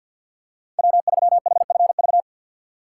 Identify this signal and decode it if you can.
Morse code, heard as a keyed tone, about 47 wpm, 700 Hz. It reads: W3HFV
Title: USA Call Signs - 5 Characters - 2x Character Spacing - 50wpm